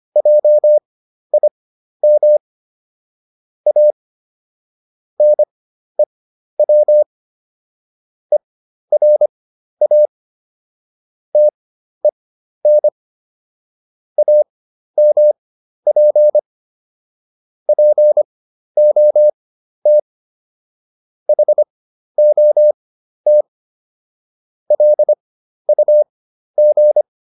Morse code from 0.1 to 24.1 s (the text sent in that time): JIM A NEW ERA TEN AMP POT HOT